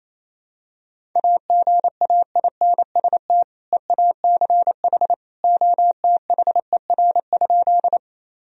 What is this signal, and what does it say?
Morse code, heard as a keyed tone, about 28 wpm, 700 Hz. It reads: AGAINST EACH OTHER?
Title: Sentences from Top 1000 Words - Review Part 1 - 28wpm